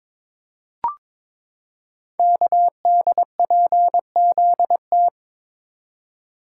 Morse code, heard as a keyed tone, about 23 wpm, 700 Hz. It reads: KDPZT
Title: Code Groups - 5 Letters - 22wpm